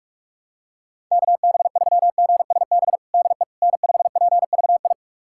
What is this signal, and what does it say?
Morse code, heard as a keyed tone, about 42 wpm, 700 Hz. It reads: KB3GSB DE N5PVI
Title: USA Callsign DE Callsign 45wpm